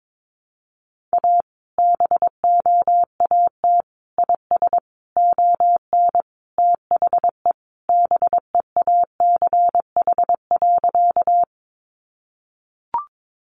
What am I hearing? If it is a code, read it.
Morse code, heard as a keyed tone, about 22 wpm, 700 Hz. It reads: A BOAT IS ON THE BEACH.
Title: Sentences from Top 1500 Words 22wpm